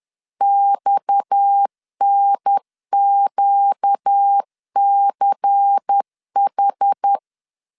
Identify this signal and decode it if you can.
Morse code, heard as a keyed tone, about 10 wpm, 800 Hz. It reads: XNQCH
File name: LEZIONI / 6 PKX0 / 6 PKX 10wpm 2426